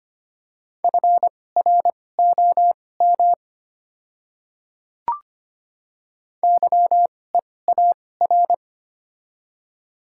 Morse code, heard as a keyed tone, about 25 wpm, spaced as for 18 wpm, 690 Hz. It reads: FROM YEAR
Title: 2x Character Spacing - Top 200 Words - 25wpm